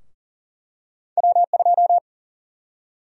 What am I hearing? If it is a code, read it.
Morse code, heard as a keyed tone, about 40 wpm, 700 Hz. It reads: W2
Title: USA Call Sign Prefixes - 40wpm